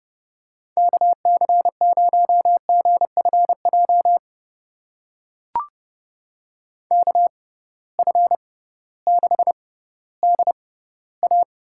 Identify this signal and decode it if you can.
Morse code, heard as a keyed tone, about 30 wpm, spaced as for 16 wpm, 700 Hz. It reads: KC0GFJ KF6DA
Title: USA Call Signs - 6 Characters - 6x Character Spacing - 30wpm